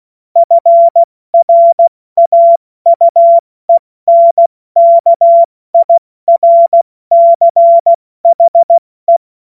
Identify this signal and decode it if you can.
Morse code, heard as a keyed tone, about 16 wpm, 680 Hz. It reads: FRAUENKIRCHE
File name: Morsekurs / Witze / Witz 20 Februar 2024 70 F80